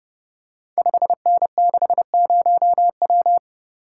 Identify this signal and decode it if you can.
Morse code, heard as a keyed tone, about 30 wpm, 700 Hz. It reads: 5N60W
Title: Code Groups - 5 Letter:Numbers - 30wpm